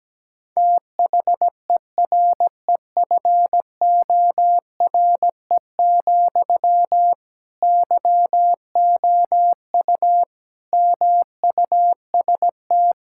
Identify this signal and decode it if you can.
Morse code, heard as a keyed tone, about 17 wpm, 700 Hz. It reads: THEREFORE, YOU MUST
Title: Sentences from Top 1000 Words - Review Part 5 - 17wpm